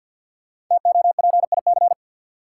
Morse code, heard as a keyed tone, about 49 wpm, 700 Hz. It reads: TOPIC
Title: Sentences from Top 1200 Words - New Word Drill Alt - 50wpm